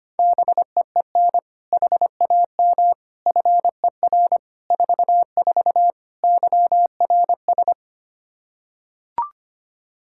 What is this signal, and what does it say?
Morse code, heard as a keyed tone, about 25 wpm, 700 Hz. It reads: BEEN HAM FER 44 YRS
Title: Practice QSO - Exchange 3 - More Info - 25wpm